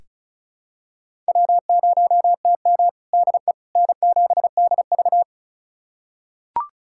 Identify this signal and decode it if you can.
Morse code, heard as a keyed tone, about 36 wpm, 700 Hz. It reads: W0TM DE N7DV
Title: USA Callsign DE Callsign - 4 Characters - 35wpm